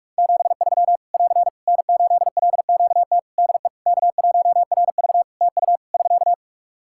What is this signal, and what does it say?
Morse code, heard as a keyed tone, about 41 wpm, 700 Hz. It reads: 73 <AR> N8LQT DE K1RV TU <SK>
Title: K1USN SST QSO Practice 45wpm